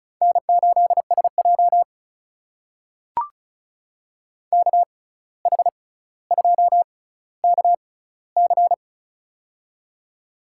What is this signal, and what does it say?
Morse code, heard as a keyed tone, about 36 wpm, spaced as for 17 wpm, 700 Hz. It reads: N8SJ KH2KC
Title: USA Call Signs - 6x Character Spacing - 35wpm